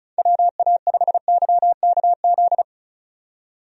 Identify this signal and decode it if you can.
Morse code, heard as a keyed tone, about 34 wpm, 700 Hz. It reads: WA5YKZ
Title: USA Call Signs - 6 Characters - 35wpm